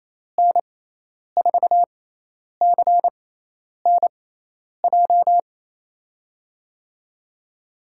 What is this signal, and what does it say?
Morse code, heard as a keyed tone, about 28 wpm, spaced as for 10 wpm, 700 Hz. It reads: N4CNJ
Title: USA Call Signs - 6 Characters - 6x Character Spacing - 28wpm